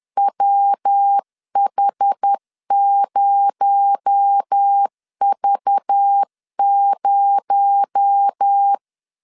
Morse code, heard as a keyed tone, about 10 wpm, 800 Hz. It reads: WH0V0
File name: CW / DET / 1 6 1500C S B 10wpm 4137